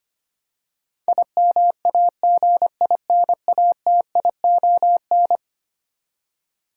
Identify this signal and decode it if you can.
Morse code, heard as a keyed tone, about 25 wpm, 700 Hz. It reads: IMAGINATION
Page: morse-code-ninja-practice-25wpm